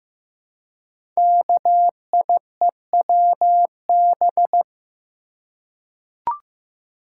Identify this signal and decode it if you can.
Morse code, heard as a keyed tone, about 15 wpm, 700 Hz. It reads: KIEWB
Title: Code Groups - 5 Letters - 15wpm